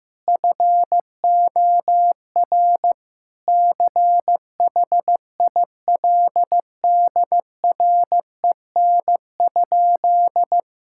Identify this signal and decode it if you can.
Morse code, heard as a keyed tone, about 16 wpm, 700 Hz. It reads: FOR CHILDREN?
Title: Sentences from Top 1000 Words - Review Part 5 - 15wpm